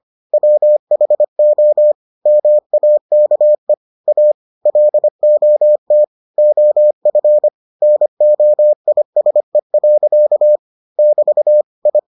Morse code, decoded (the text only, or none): WHO MAKE A LOT OF NOISE. <BT> I